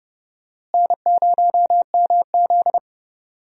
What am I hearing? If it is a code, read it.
Morse code, heard as a keyed tone, about 29 wpm, 700 Hz. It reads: N0MZ
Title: USA Call Signs - 4 Characters - Repeated 2x - 30wpm